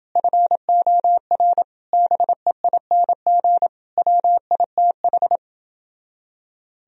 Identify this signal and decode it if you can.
Morse code, heard as a keyed tone, about 29 wpm, 700 Hz. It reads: FOR BEING WITH